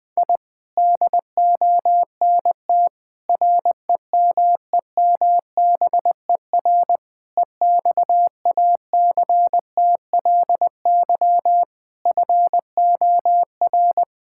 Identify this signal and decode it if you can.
Morse code, heard as a keyed tone, about 20 wpm, 700 Hz. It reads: I DONT REMEMBER EXACTLY FOR
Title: Sentences from Top 1000 Words - Review Part 6 - 20wpm